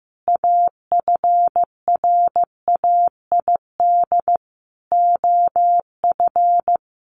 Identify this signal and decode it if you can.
Morse code, heard as a keyed tone, about 15 wpm, 700 Hz. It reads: AFRAID OF
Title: Sentences from Top 1500 Words 15wpm